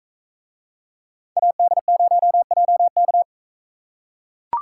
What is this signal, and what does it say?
Morse code, heard as a keyed tone, about 41 wpm, 700 Hz. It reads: AD0JK